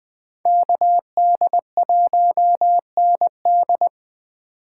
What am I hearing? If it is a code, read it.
Morse code, heard as a keyed tone, about 24 wpm, 700 Hz. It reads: KD1ND
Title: Speed Racing - USA 2x2 Call Signs 20wpm